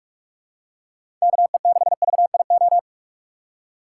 Morse code, heard as a keyed tone, about 46 wpm, 700 Hz. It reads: KE6VIO
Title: USA Call Signs - 6 Characters - 4x Character Spacing - 45wpm